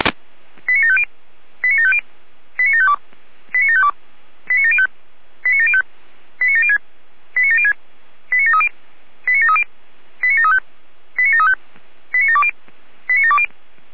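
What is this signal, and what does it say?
Five-tone selective call (ZVEI), heard as quick runs of five tones, tones of about 70 ms. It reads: 89750 89750 89752 89752 89705 89705 89706 89706 89730 89730 89735 89735 89720 89720
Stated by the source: ZVEI five-tone selective call received on SDR.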